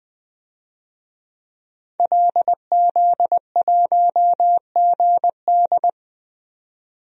Morse code, heard as a keyed tone, about 20 wpm, 700 Hz. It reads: LZ1GD
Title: DXCC-100 Call Signs - 5 Characters - 20wpm